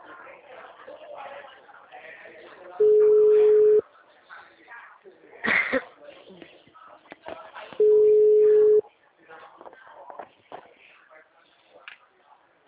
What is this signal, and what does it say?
Ringback tone (Europe, 425 Hz), with speech.
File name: out- -2016- - - .4805